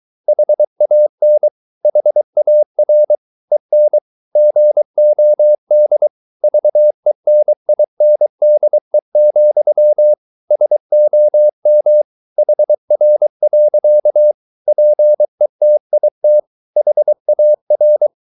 Morse code, heard as a keyed tone, about 23 wpm, 600 Hz. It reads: HAN HAR EN GOD VENINDE, SOM HR. PETIT HAR
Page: Ferie 23 wpm